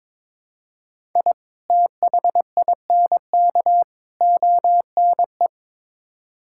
Morse code, heard as a keyed tone, about 22 wpm, 700 Hz. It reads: I THINK ONE
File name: practice / 22wpm / Build A Phrase - 5-Word Phrases - 22wpm